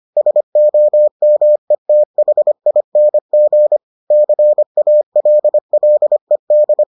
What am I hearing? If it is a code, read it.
Morse code, heard as a keyed tone, about 25 wpm, 600 Hz. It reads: SOMETHING CALLED